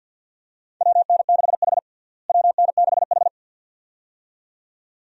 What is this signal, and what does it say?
Morse code, heard as a keyed tone, about 47 wpm, 700 Hz. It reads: WN6H WN6H
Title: USA Call Signs - 4 Characters - Repeated 2x - 50wpm